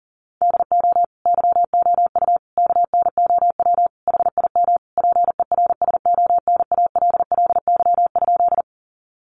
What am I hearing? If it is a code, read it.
Morse code, heard as a keyed tone, about 40 wpm, 700 Hz. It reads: DO YOU KNOW HIM PERSONALLY?
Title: Sentences from Top 1400 Words 40wpm